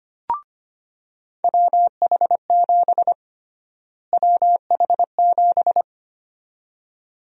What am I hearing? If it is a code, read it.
Morse code, heard as a keyed tone, about 25 wpm, 700 Hz. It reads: WH7 WH7
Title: USA Call Sign Prefixes - Repeated 2x - 25wpm